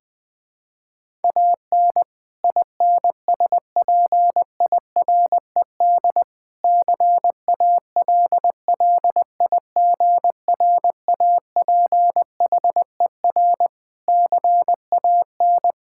Morse code, heard as a keyed tone, about 20 wpm, 700 Hz. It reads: AN INSPIRED CALLIGRAPHER CAN
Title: Pangram Challenge v2 - 20wpm